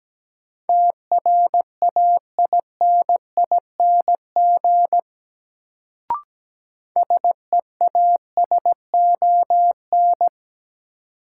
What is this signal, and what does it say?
Morse code, heard as a keyed tone, about 17 wpm, 700 Hz. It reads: TRAINING SEASON
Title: Sentences from Top 1400 Words - New Word Drill Alt - 17wpm